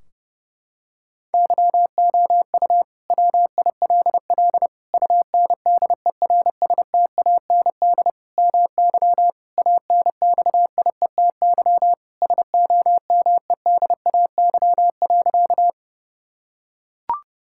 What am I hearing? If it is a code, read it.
Morse code, heard as a keyed tone, about 30 wpm, 700 Hz. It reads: YOU WILL UNDERSTAND MY ANXIETY SOMEDAY.
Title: Sentences from Top 1600 Words 30wpm